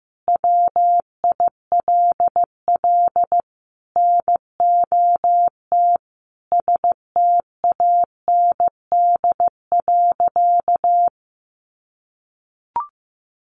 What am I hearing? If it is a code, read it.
Morse code, heard as a keyed tone, about 15 wpm, 700 Hz. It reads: WILL NOT STAND.
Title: Sentences from Top 1300 Words 15wpm